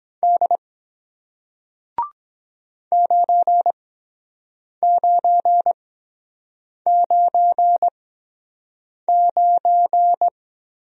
Morse code, heard as a keyed tone, about 23 wpm, spaced as for 7 wpm, 700 Hz. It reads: D 9999